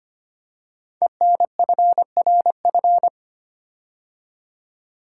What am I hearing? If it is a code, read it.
Morse code, heard as a keyed tone, about 25 wpm, 700 Hz. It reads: ENFRF
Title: Code Groups - 5 Letters - 25wpm